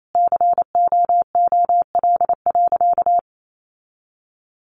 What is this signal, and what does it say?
Morse code, heard as a keyed tone, about 28 wpm, 700 Hz. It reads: COOL.
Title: Sentences from Top 1200 Words 28wpm